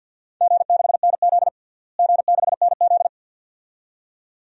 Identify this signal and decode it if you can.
Morse code, heard as a keyed tone, about 46 wpm, 700 Hz. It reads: G6NZ G6NZ
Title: DXCC-100 Call Signs - 4 Characters - Repeated 2x - 50wpm